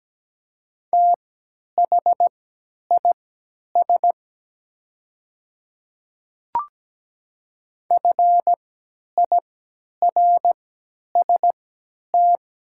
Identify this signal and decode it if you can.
Morse code, heard as a keyed tone, about 17 wpm, spaced as for 10 wpm, 700 Hz. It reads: THIS FIRST
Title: 3x Character Spacing - Top 100 Words - 17wpm